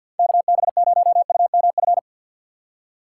Morse code, heard as a keyed tone, about 50 wpm, 700 Hz. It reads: KB0UMF
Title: USA Call Signs - 6 Characters - 3x Character Spacing - 50wpm